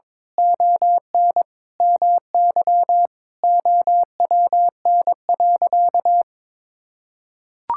Morse code, heard as a keyed tone, about 22 wpm, 700 Hz. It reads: ON MY OWN.
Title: Sentences from Top 1000 Words - Review Part 6 - 22wpm